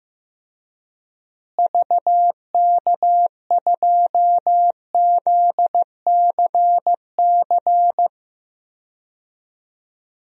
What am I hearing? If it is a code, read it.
Morse code, heard as a keyed tone, about 15 wpm, 700 Hz. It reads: VK2ZCC
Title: DXCC-100 Call Signs - 6 Characters - Repeated 2x - 15wpm